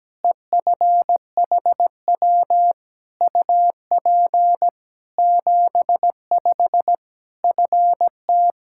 Morse code, heard as a keyed tone, about 17 wpm, 700 Hz. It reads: EFHW UP 75 FT
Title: Practice QSO - Exchange 2 - Info - 17wpm